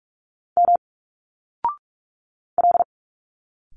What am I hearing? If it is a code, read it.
Morse code, heard as a keyed tone, about 46 wpm, 700 Hz. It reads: M L
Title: Single Letters 45wpm